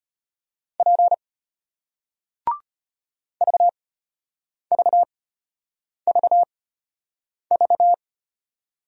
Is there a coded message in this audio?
Yes: Morse code from 0.8 to 7.9 s, heard as a keyed tone, about 35 wpm, spaced as for 8 wpm, 700 Hz. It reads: P VVVV